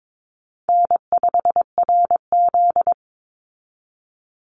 Morse code, heard as a keyed tone, about 22 wpm, 700 Hz. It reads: N5RZ
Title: Call Signs Repeated 2x - 22wpm